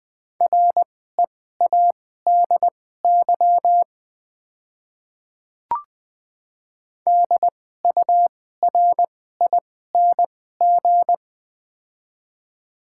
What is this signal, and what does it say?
Morse code, heard as a keyed tone, about 20 wpm, spaced as for 14 wpm, 700 Hz. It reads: READY DURING
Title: 2x Character Spacing - Top 500 Words - 20wpm